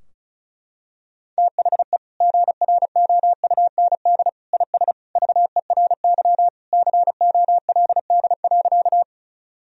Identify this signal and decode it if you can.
Morse code, heard as a keyed tone, about 36 wpm, 700 Hz. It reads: THE GROUND IS VERY COLD.
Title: Sentences from Top 1000 Words - Review Part 2 - 35wpm